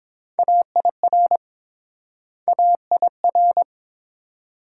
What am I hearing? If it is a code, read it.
Morse code, heard as a keyed tone, about 30 wpm, 700 Hz. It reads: AIR AIR